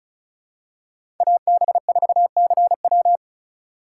Morse code, heard as a keyed tone, about 36 wpm, 700 Hz. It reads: AB4CW
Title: USA Call Signs - 5 Characters - 2x Character Spacing - 35wpm